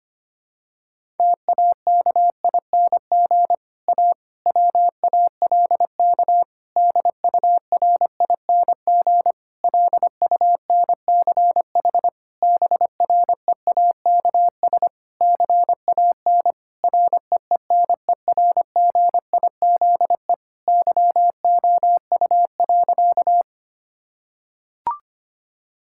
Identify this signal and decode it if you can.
Morse code, heard as a keyed tone, about 25 wpm, 700 Hz. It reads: TAKING A WALK DURING LUNCH BREAKS CAN REENERGIZE YOU.
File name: practice / 25wpm / Difficult Sentences with Hints - 25wpm